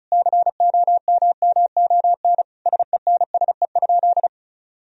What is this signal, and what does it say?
Morse code, heard as a keyed tone, about 34 wpm, 700 Hz. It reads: COMMON SENSE?
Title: Sentences from Top 1000 Words - Review Part 4 - 35wpm